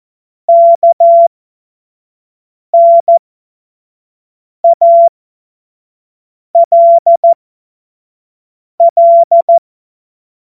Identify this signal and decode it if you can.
Morse code, heard as a keyed tone, about 14 wpm, spaced as for 5 wpm, 680 Hz. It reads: KNALL
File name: Morsekurs / Witze / Witz 08. November 2022 25 F70